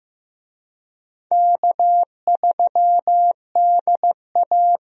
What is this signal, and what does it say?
Morse code, heard as a keyed tone, about 15 wpm, 700 Hz. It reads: K3DA
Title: USA Call Signs - 4 Characters - Repeated 3x - 15wpm